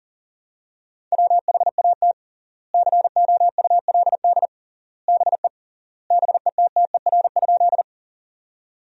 Morse code, heard as a keyed tone, about 40 wpm, 700 Hz. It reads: WHAT COULD BE BETTER?
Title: 3x Word-Spacing - Sentences from Top 400 Words - 40wpm